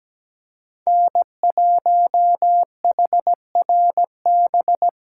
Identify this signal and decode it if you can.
Morse code, heard as a keyed tone, about 20 wpm, 700 Hz. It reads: N1HRB
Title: Speed Racing - USA 1x3 Call Signs 17wpm